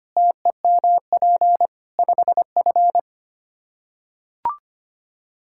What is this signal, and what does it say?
Morse code, heard as a keyed tone, about 25 wpm, 700 Hz. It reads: TEMP 5F
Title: Practice QSO - Exchange 2 - Info - 25wpm